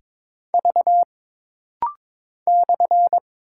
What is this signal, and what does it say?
Morse code, heard as a keyed tone, about 22 wpm, 700 Hz. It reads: V /
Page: This rapid-fire timing is geared to push your ICR (Instant Character Recognition) skills.